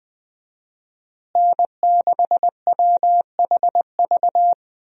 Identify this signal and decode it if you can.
Morse code, heard as a keyed tone, about 21 wpm, 700 Hz. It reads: N6WHV
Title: USA Call Signs - 5 Characters - Repeated 2x - 20wpm